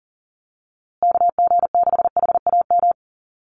Morse code, heard as a keyed tone, about 40 wpm, 700 Hz. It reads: KG6HAM
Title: USA Call Signs Repeated 2x 40wpm